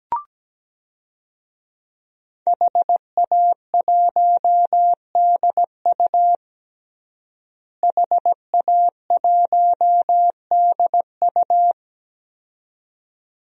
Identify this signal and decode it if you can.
Morse code, heard as a keyed tone, about 17 wpm, 700 Hz. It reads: HA1DU HA1DU
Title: DXCC-100 Call Signs - 5 Characters - Repeated 2x - 17wpm